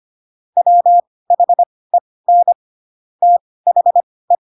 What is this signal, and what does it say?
Morse code, heard as a keyed tone, about 25 wpm, spaced as for 18 wpm, 700 Hz. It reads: WHEN THE
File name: Pinocchio / Pinocchio ch2 12 25 ik1hge